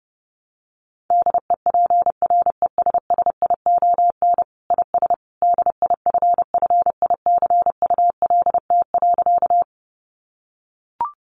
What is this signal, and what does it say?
Morse code, heard as a keyed tone, about 30 wpm, 700 Hz. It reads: DEPRESSION IS DIFFICULT.
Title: Sentences from Top 1300 Words 30wpm